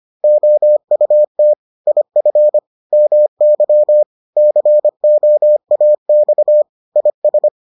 Morse code, heard as a keyed tone, about 25 wpm, 600 Hz. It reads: OUT IF MY COAX IS